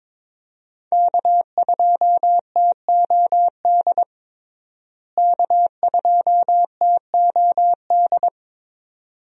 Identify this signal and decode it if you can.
Morse code, heard as a keyed tone, about 22 wpm, 700 Hz. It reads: K2TOD K2TOD
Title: USA Call Signs - 5 Characters - Repeated 2x - 22wpm